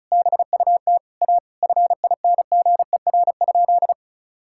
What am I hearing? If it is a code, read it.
Morse code, heard as a keyed tone, about 34 wpm, 700 Hz. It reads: BUT A FINGER?
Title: Sentences from Top 1600 Words 35wpm